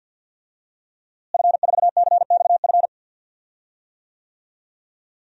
Morse code, heard as a keyed tone, about 46 wpm, 700 Hz. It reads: F4CXF
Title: DXCC-100 Call Signs - 5 Characters - Repeated 3x - 50wpm